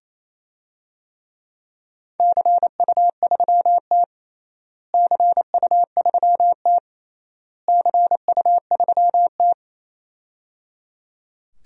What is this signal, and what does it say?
Morse code, heard as a keyed tone, about 27 wpm, 700 Hz. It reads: CU3T CU3T CU3T
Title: DXCC-100 Call Signs - 4 Characters - Repeated 3x - 28wpm